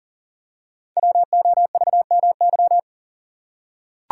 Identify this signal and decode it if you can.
Morse code, heard as a keyed tone, about 40 wpm, 700 Hz. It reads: WOVMY
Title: Code Groups - 5 Letters - 40wpm